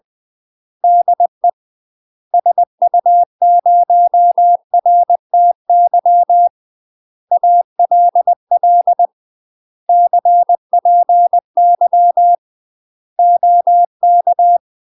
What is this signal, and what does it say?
Morse code, heard as a keyed tone, about 20 wpm, 700 Hz. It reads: DE SU0RTY ALL CPY OK
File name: UK MASTER QSOs TRAINING / 20WPM / UK MASTER QSOs 0042